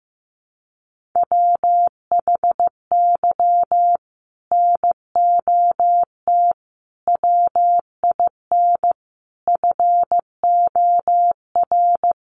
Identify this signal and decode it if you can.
Morse code, heard as a keyed tone, about 15 wpm, 700 Hz. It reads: WHY NOT WIN FOR
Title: Sentences from Top 1500 Words 15wpm